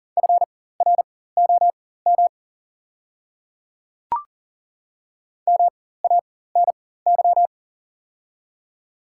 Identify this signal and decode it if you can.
Morse code, heard as a keyed tone, about 40 wpm, spaced as for 19 wpm, 700 Hz. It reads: FROM MANY
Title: 4x Character Spacing - Top 100 Words - 40wpm